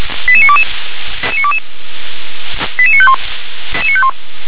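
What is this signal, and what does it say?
Five-tone selective call (ZVEI), heard as quick runs of five tones, tones of about 70 ms. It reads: 88922 88951
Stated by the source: ZVEI five-tone selective call received via SDR.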